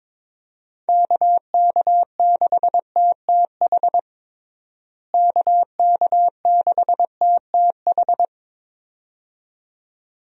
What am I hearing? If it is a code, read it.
Morse code, heard as a keyed tone, about 22 wpm, 700 Hz. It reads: KK6TTH KK6TTH
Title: USA Call Signs - 6 Characters - Repeated 2x - 22wpm